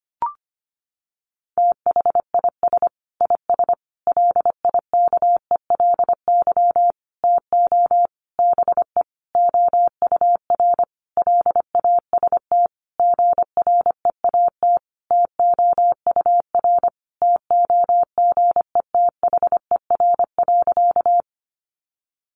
Morse code, heard as a keyed tone, about 25 wpm, 700 Hz. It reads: THIS IS LIKELY TO BE OUR LAST GREAT TOUR TOGETHER.
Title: Sentences from Top 1400 Words 25wpm